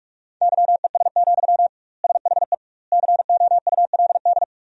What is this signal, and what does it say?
Morse code, heard as a keyed tone, about 46 wpm, 700 Hz. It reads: YES, SHE COULD
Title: 2x Word-Spacing - Sentences from Top 500 Words - 45wpm